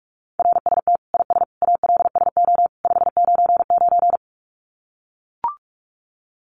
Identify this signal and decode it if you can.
Morse code, heard as a keyed tone, about 44 wpm, 700 Hz. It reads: RST IS ALSO 599
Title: Common QSO Phrases 45wpm